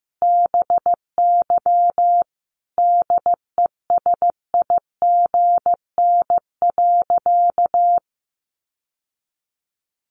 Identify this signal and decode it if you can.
Morse code, heard as a keyed tone, about 15 wpm, 700 Hz. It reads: BY DESIGN.
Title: Sentences from Top 800 Words 15wpm